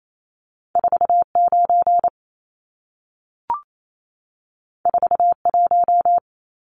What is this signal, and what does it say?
Morse code, heard as a keyed tone, about 28 wpm, 700 Hz. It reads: 49 41
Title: Numbers - 2-Digit 28wpm